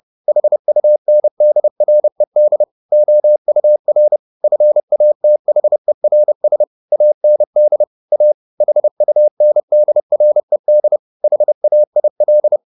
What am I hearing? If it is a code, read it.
Morse code, heard as a keyed tone, about 30 wpm, 600 Hz. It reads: HUNDRED OUR FATHERS AND A HUNDRED HAIL